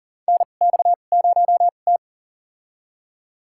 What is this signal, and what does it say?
Morse code, heard as a keyed tone, about 40 wpm, spaced as for 29 wpm, 700 Hz. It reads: NX0T
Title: USA Call Signs - 4 Characters - 2x Character Spacing - 40wpm